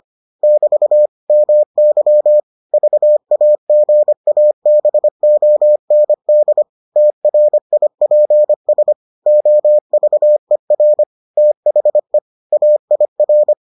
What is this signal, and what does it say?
Morse code, heard as a keyed tone, about 25 wpm, 600 Hz. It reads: <BT> MY VAGABOND TRIPS OVER THE AIR